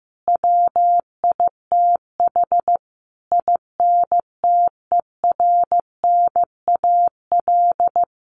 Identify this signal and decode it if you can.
Morse code, heard as a keyed tone, about 15 wpm, 700 Hz. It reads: WITH INTERNAL
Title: Sentences from Top 1400 Words 15wpm